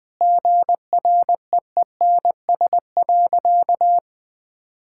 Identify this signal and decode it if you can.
Morse code, heard as a keyed tone, about 21 wpm, 700 Hz. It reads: GREENS.
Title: Easy Sentences with Hints - 20wpm